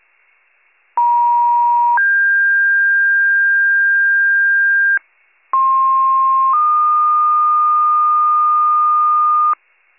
Two-tone page, heard as a one-second tone, then a three-second tone, about 950 then 1600 Hz, about 1060 then 1220 Hz, with noise. It is heard from 1.0 to 9.5 s.